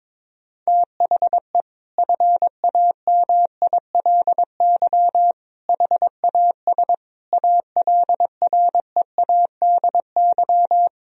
Morse code, heard as a keyed tone, about 22 wpm, 700 Hz. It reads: THE FAMILY HAS ALREADY
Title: Sentences from Top 1000 Words - Review Part 2 - 22wpm